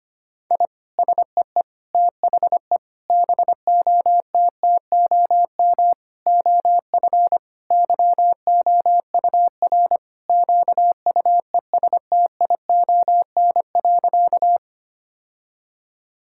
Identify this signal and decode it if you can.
Morse code, heard as a keyed tone, about 25 wpm, 700 Hz. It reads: I SEE THE BOTTOM OF YOUR QUESTION.
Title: Sentences from Top 1000 Words - Review Part 6 - 25wpm